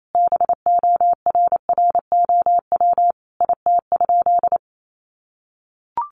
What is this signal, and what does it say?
Morse code, heard as a keyed tone, about 28 wpm, 700 Hz. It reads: BORROW IT?
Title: Sentences from Top 1500 Words 28wpm